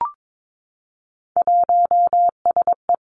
Morse code, heard as a keyed tone, about 22 wpm, 700 Hz. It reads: 1SE
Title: Three Challenging Letter-Number Combinations 22wpm